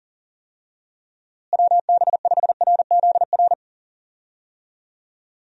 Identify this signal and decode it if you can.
Morse code, heard as a keyed tone, about 40 wpm, 700 Hz. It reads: WB5RZR
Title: USA Call Signs - 6 Characters - Repeated 3x - 40wpm